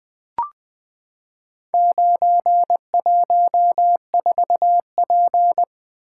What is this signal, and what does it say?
Morse code, heard as a keyed tone, about 20 wpm, 700 Hz. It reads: 914P
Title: Four Letter-Number Combinations - 20wpm